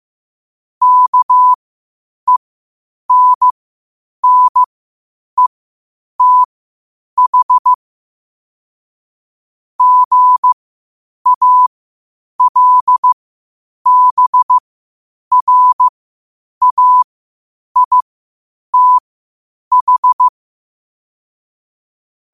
Morse code, heard as a keyed tone, about 15 wpm, spaced as for 8 wpm, 1000 Hz. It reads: KENNETH GALBRAITH
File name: qotdfiles / 8wpm